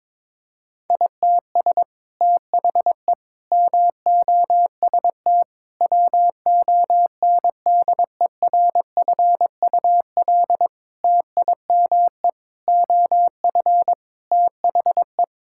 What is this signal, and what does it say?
Morse code, heard as a keyed tone, about 22 wpm, 700 Hz. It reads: ITS THE MOST WONDERFUL TIME OF THE
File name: practice / 22wpm / Sets of 8 Words - 22wpm